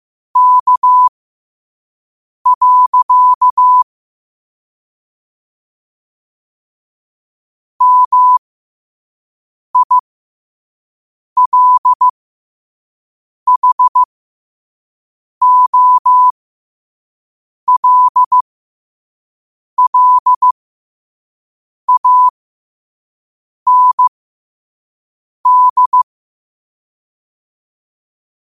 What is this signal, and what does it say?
Morse code, heard as a keyed tone, about 15 wpm, spaced as for 5 wpm, 1000 Hz. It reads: K. MILHOLLAND
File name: qotdfiles / 5wpm